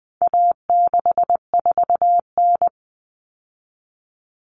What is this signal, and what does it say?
Morse code, heard as a keyed tone, about 20 wpm, 700 Hz. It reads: A64N
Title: Common Cut Numbers - 4-Digit 20wpm